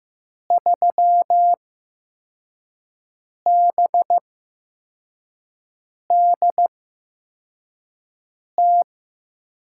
Morse code, heard as a keyed tone, about 15 wpm, spaced as for 4 wpm, 700 Hz. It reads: 3BDT
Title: USA Call Signs - 5 Characters - 8x Character Spacing - 15wpm